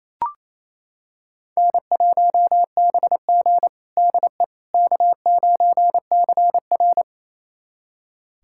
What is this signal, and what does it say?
Morse code, heard as a keyed tone, about 28 wpm, 700 Hz. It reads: N1BG DE K9CR
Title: USA Callsign DE Callsign - 4 Characters - 28wpm